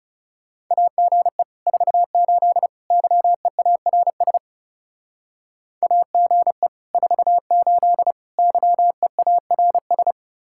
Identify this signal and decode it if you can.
Morse code, heard as a keyed tone, about 40 wpm, 700 Hz. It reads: AGE 48 YEARS AGE 48 YEARS